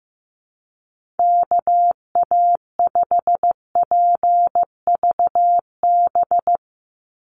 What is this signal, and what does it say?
Morse code, heard as a keyed tone, about 15 wpm, 700 Hz. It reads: KA5PVB
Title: NA SOTA Chaser Callsigns 15wpm